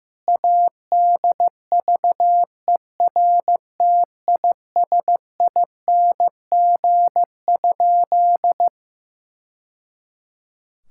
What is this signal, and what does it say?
Morse code, heard as a keyed tone, about 15 wpm, 700 Hz. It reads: ADVERTISING?
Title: Sentences from Top 1000 Words - Review Part 6 - 15wpm